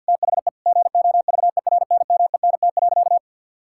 Morse code, heard as a keyed tone, about 50 wpm, 700 Hz. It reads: THE GOVERNMENT.
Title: Sentences from Top 1000 Words - Review Part 6 - 50wpm